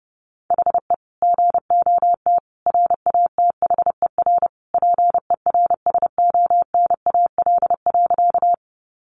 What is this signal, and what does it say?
Morse code, heard as a keyed tone, about 30 wpm, 700 Hz. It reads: HE GOT RATHER PERSONAL.
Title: Sentences from Top 700 Words 30wpm